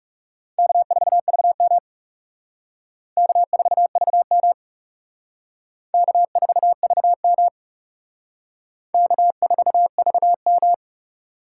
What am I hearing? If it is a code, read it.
Morse code, heard as a keyed tone, about 37 wpm, 700 Hz. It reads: K4VM K4VM K4VM K4VM
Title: Speed Racing - USA 1x2 Call Signs 30wpm